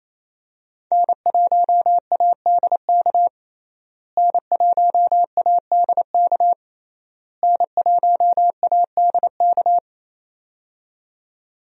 Morse code, heard as a keyed tone, about 28 wpm, 700 Hz. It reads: N1ADK N1ADK N1ADK
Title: USA Call Signs - 5 Characters - Repeated 3x - 28wpm